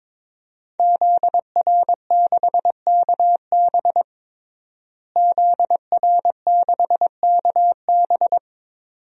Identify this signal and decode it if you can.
Morse code, heard as a keyed tone, about 22 wpm, 700 Hz. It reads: ZR6KB ZR6KB
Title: DXCC-100 Call Signs - 5 Characters - Repeated 2x - 22wpm